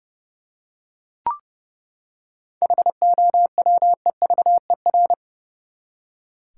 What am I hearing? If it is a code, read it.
Morse code, heard as a keyed tone, about 30 wpm, 700 Hz. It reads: HOWEVER
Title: Seven Letter Words - 30wpm